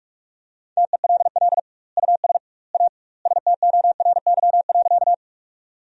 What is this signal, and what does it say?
Morse code, heard as a keyed tone, about 43 wpm, 700 Hz. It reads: TELL US A STORY.
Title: 2x Word-Spacing - Sentences from Top 300 Words - 45wpm